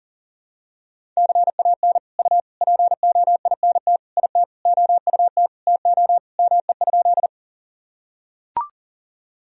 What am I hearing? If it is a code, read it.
Morse code, heard as a keyed tone, about 40 wpm, 700 Hz. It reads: CAN U POINT IT OUT TO ME?
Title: Sentences from Top 200 Words and Abbreviations 40wpm